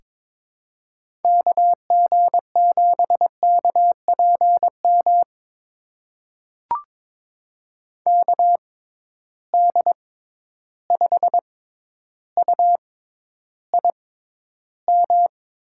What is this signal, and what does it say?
Morse code, heard as a keyed tone, about 22 wpm, spaced as for 11 wpm, 700 Hz. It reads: KG7KPM KD5UIM
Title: USA Call Signs - 6 Characters - 6x Character Spacing - 22wpm